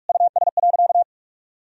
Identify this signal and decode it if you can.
Morse code, heard as a keyed tone, about 46 wpm, 700 Hz. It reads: US.
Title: Sentences from Top 1000 Words - Review Part 3 - 45wpm